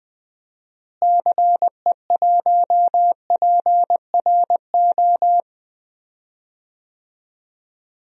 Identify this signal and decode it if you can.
Morse code, heard as a keyed tone, about 20 wpm, 700 Hz. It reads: CE1PRO
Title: DXCC-100 Call Signs - 6 Characters - Repeated 2x - 20wpm